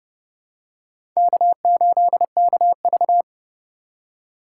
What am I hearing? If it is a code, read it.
Morse code, heard as a keyed tone, about 30 wpm, 700 Hz. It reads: K8KV